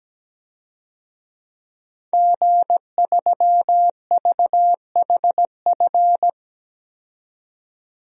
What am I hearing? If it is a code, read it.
Morse code, heard as a keyed tone, about 17 wpm, 700 Hz. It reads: G3VHF
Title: DXCC-100 Call Signs - 5 Characters - 17wpm